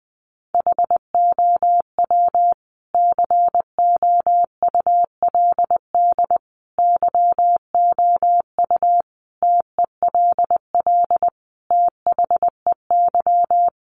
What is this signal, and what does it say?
Morse code, heard as a keyed tone, about 19 wpm, 700 Hz. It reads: HOW COULD YOU TELL THEY
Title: Sentences from Top 1000 Words 20wpm